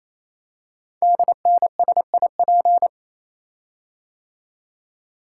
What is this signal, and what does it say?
Morse code, heard as a keyed tone, about 28 wpm, 700 Hz. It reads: DNSIP
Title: Code Groups - 5 Letters - 28wpm